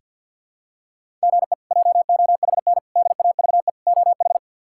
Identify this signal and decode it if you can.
Morse code, heard as a keyed tone, about 46 wpm, 700 Hz. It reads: GE JOHN DAVE OH
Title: K1USN SST QSO Practice - 50wpm